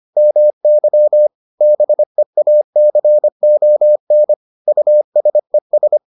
Morse code, heard as a keyed tone, about 25 wpm, 600 Hz. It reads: MY BEACON USES